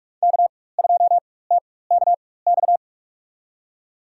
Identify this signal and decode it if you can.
Morse code, heard as a keyed tone, about 43 wpm, spaced as for 21 wpm, 700 Hz. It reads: K2TKX
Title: USA Call Signs - 5 Characters - 4x Character Spacing - 45wpm